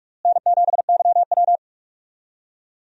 Morse code, heard as a keyed tone, about 46 wpm, 700 Hz. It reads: N7YW
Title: USA Call Signs - 4 Characters - Repeated 3x - 45wpm